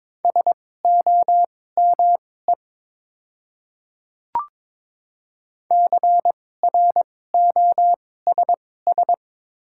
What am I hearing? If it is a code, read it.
Morse code, heard as a keyed tone, about 22 wpm, spaced as for 16 wpm, 700 Hz. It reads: SOME CROSS
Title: 2x Character Spacing - Top 300 Words - 22wpm